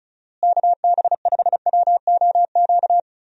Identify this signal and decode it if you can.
Morse code, heard as a keyed tone, about 36 wpm, 700 Hz. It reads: KB5WOQ
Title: USA Call Signs - 6 Characters - Repeated 3x - 35wpm